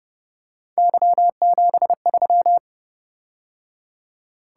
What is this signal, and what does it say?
Morse code, heard as a keyed tone, about 29 wpm, 700 Hz. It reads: Y73